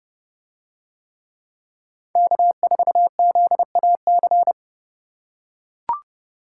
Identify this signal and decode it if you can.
Morse code, heard as a keyed tone, about 29 wpm, 700 Hz. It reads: K4ZAC